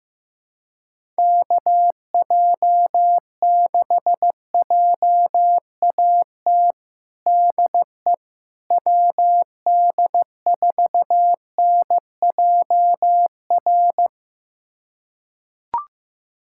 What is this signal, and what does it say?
Morse code, heard as a keyed tone, about 15 wpm, 700 Hz. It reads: KJ6JAT DE WD4NJR
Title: USA Callsign DE Callsign - 6 Characters - 15wpm